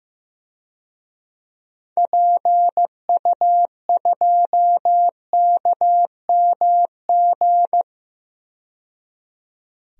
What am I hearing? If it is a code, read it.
Morse code, heard as a keyed tone, about 15 wpm, 700 Hz. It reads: PU2KMG
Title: DXCC-100 Call Signs - 6 Characters - 15wpm